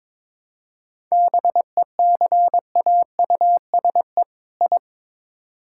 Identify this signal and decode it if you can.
Morse code, heard as a keyed tone, about 22 wpm, 700 Hz. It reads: BECAUSE I
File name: Build A Phrase - 2-Word Phrases - 22wpm